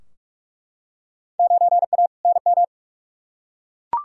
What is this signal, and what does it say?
Morse code, heard as a keyed tone, about 44 wpm, 700 Hz. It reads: 9A NM
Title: Field Day Exchanges Encore - 45wpm